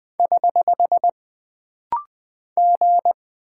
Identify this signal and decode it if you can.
Morse code, heard as a keyed tone, about 20 wpm, 700 Hz. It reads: <HH> G
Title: Single Character (Letter:Number:Punctuation:Prosign) - Rapid-Fire - 20wpm